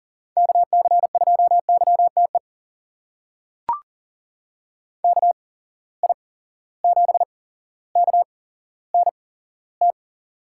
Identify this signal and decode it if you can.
Morse code, heard as a keyed tone, about 40 wpm, spaced as for 17 wpm, 700 Hz. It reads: KC2YTE KI7KNT
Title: USA Call Signs - 6 Characters - 8x Character Spacing - 40wpm